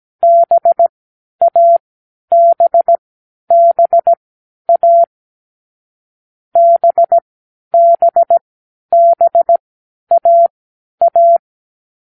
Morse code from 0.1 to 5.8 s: BABBA